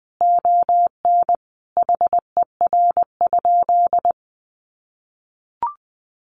Morse code, heard as a keyed tone, about 20 wpm, 700 Hz. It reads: ON HER?
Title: Sentences from Top 1500 Words 20wpm